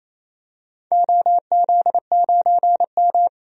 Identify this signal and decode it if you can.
Morse code, heard as a keyed tone, about 28 wpm, 700 Hz. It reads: OZ9M
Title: DXCC-100 Call Signs - 4 Characters - 28wpm